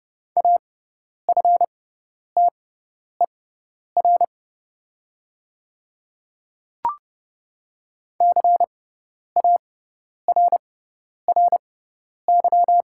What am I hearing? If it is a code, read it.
Morse code, heard as a keyed tone, about 29 wpm, spaced as for 10 wpm, 700 Hz. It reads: AFTER CARRY
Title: 6x Character Spacing - Top 300 Words - 30wpm